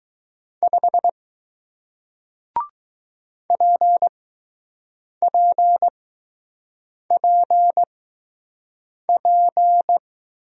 Morse code, heard as a keyed tone, about 20 wpm, spaced as for 6 wpm, 700 Hz. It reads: 5 PPPP